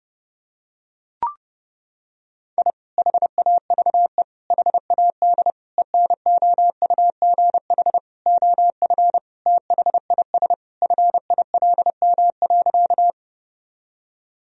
Morse code, heard as a keyed tone, about 32 wpm, 700 Hz. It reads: I HAVE HAD ENOUGH OF THIS FILM.
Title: Sentences from Top 1000 Words - Review Part 3 - 30wpm